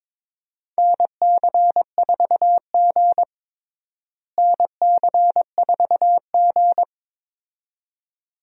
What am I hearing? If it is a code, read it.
Morse code, heard as a keyed tone, about 22 wpm, 700 Hz. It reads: NC4G NC4G
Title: USA Call Signs - 4 Characters - Repeated 2x - 22wpm